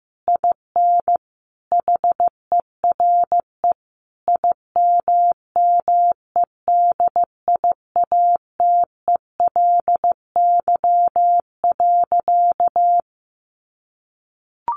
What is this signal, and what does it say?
Morse code, heard as a keyed tone, about 15 wpm, 700 Hz. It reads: IN HERE IMMEDIATELY.
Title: Sentences from Top 1100 Words 15wpm